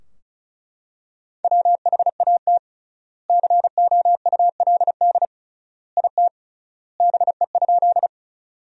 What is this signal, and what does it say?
Morse code, heard as a keyed tone, about 36 wpm, 700 Hz. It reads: WHAT COULD IT BE?
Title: 3x Word-Spacing - Sentences from Top 100 Words - 35wpm